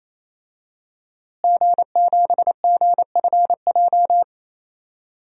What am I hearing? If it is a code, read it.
Morse code, heard as a keyed tone, about 28 wpm, 700 Hz. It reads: G7GFJ
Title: DXCC-100 Call Signs - 5 Characters - Repeated 3x - 28wpm